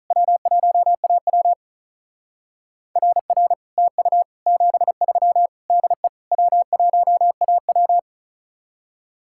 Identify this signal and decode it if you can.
Morse code, heard as a keyed tone, about 43 wpm, 700 Hz. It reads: W1AW RR TU 73 DE W1AW
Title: Speed-Racing Common QSO Phrases - 35wpm